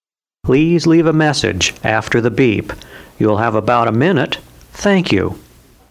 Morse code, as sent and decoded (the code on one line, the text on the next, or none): none